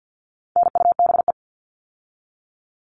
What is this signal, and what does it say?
Morse code, heard as a keyed tone, about 48 wpm, 700 Hz. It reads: NUBE
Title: All Cut Numbers - 4-Digit 50wpm